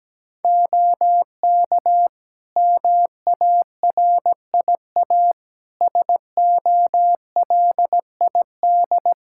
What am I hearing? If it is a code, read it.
Morse code, heard as a keyed tone, about 17 wpm, 700 Hz. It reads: OK MARIA SOLID
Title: Practice QSO - Exchange 3 - More Info - 17wpm